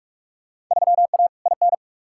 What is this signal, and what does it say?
Morse code, heard as a keyed tone, about 46 wpm, 700 Hz. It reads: 3A IN
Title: Field Day Exchanges - 45wpm